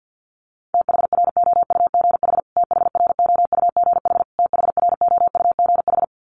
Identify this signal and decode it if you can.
Morse code, heard as a keyed tone, about 50 wpm, 700 Hz. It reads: THROUGH THROUGH THROUGH
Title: Top 500 Words Repeated 50wpm